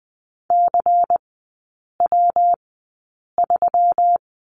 Morse code, heard as a keyed tone, about 20 wpm, spaced as for 8 wpm, 700 Hz. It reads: CW3
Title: 127 Lesson 21e - Number 6 - Three Characters 20wpm